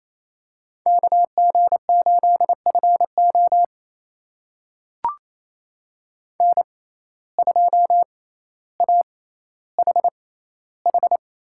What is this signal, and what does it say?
Morse code, heard as a keyed tone, about 28 wpm, spaced as for 14 wpm, 700 Hz. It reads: KG8FO N2AHH
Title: USA Call Signs - 5 Characters - 6x Character Spacing - 28wpm